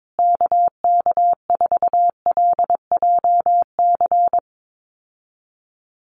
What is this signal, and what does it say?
Morse code, heard as a keyed tone, about 22 wpm, 700 Hz. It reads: KK4LJC
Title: USA Call Signs Repeated 3x 22wpm